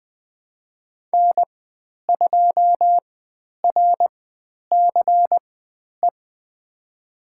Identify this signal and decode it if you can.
Morse code, heard as a keyed tone, about 20 wpm, spaced as for 10 wpm, 700 Hz. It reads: N2RCE
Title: 095 Lesson 16g - Letter c - Call Signs 20wpm